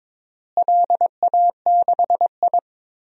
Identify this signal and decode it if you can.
Morse code, heard as a keyed tone, about 22 wpm, 700 Hz. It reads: LA6I
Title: DXCC-100 Call Signs - 4 Characters - Repeated 3x - 22wpm